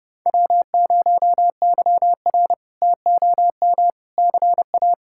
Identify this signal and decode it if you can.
Morse code, heard as a keyed tone, about 30 wpm, 700 Hz. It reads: W0YR TOM CA
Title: K1USN SST QSO Practice - 30wpm